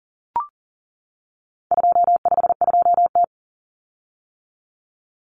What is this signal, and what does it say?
Morse code, heard as a keyed tone, about 40 wpm, 700 Hz. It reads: 252T
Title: Common Cut Numbers - 4-Digit 40wpm